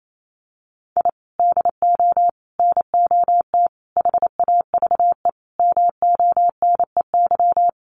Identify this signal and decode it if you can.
Morse code, heard as a keyed tone, about 28 wpm, 700 Hz. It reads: I DO NOT HAVE MONEY
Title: Sentences from Top 1400 Words 28wpm